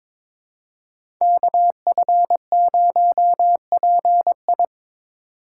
Morse code, heard as a keyed tone, about 22 wpm, 700 Hz. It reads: KF0PI
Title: USA Call Signs - 5 Characters - 3x Character Spacing - 22wpm